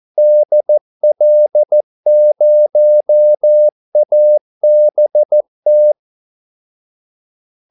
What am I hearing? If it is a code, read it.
Morse code, heard as a keyed tone, about 14 wpm, 600 Hz. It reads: DL0ABT
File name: Callsigns-14wpm